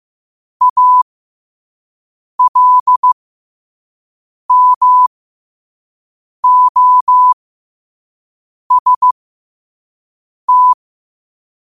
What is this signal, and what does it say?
Morse code, heard as a keyed tone, about 15 wpm, spaced as for 5 wpm, 1000 Hz. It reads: ALMOST
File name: qotdfiles / 5wpm